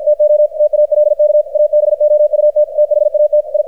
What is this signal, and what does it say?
Morse code, heard as a keyed tone, about 51 wpm, 600 Hz. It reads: TO TALK ABOUT THAT I
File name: archived / CRSnet